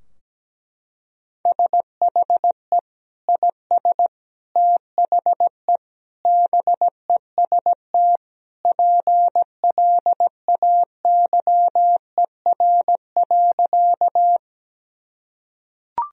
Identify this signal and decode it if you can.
Morse code, heard as a keyed tone, about 17 wpm, 700 Hz. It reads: SHE IS THE BEST PLAYER.
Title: Sentences from Top 1000 Words - Review Part 4 - 17wpm